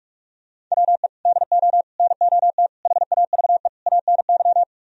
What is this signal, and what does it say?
Morse code, heard as a keyed tone, about 49 wpm, 700 Hz. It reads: WE DO NOT HAVE ANY